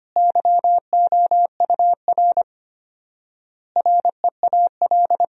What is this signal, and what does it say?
Morse code, heard as a keyed tone, about 25 wpm, 700 Hz. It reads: YOUR REAL
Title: 4x Word-Spacing - Sentences from Top 300 Words - 25wpm